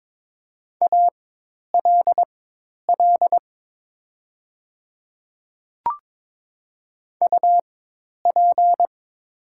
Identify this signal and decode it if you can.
Morse code, heard as a keyed tone, about 22 wpm, spaced as for 10 wpm, 700 Hz. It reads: ALL UP
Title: 4x Character Spacing - Top 100 Words - 22wpm